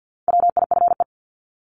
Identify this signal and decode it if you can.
Morse code, heard as a keyed tone, about 50 wpm, 700 Hz. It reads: WIFE
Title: Top 1751-2000 Words 50wpm